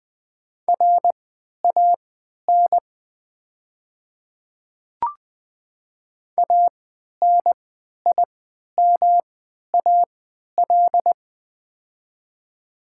Morse code, heard as a keyed tone, about 20 wpm, spaced as for 11 wpm, 700 Hz. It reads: RAN ANIMAL
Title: 3x Character Spacing - Top 500 Words - 20wpm